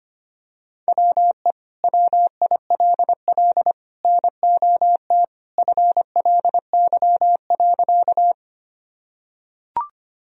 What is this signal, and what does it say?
Morse code, heard as a keyed tone, about 25 wpm, 700 Hz. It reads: WE WILL NOT FLY.
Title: Sentences from Top 1000 Words - Review Part 2 - 25wpm